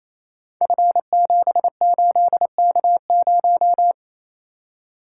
Morse code, heard as a keyed tone, about 28 wpm, 700 Hz. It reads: F78K0
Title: Code Groups - 5 Letter:Numbers - 28wpm